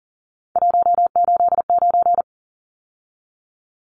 Morse code, heard as a keyed tone, about 40 wpm, 700 Hz. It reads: 189
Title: Numbers - 3-Digit 40wpm